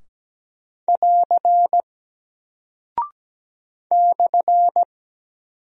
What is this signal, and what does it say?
Morse code, heard as a keyed tone, about 17 wpm, 700 Hz. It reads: <AR> /
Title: Single Character (Letter:Number:Punctuation:Prosign) Alt 17wpm